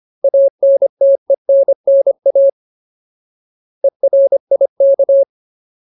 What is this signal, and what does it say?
Morse code, heard as a keyed tone, about 25 wpm, 550 Hz. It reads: ANTENNA ERIK
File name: Ultimate Word List / 25WPM Ultimate